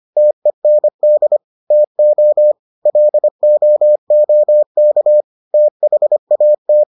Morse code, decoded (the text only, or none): TEND TO LOOK THAT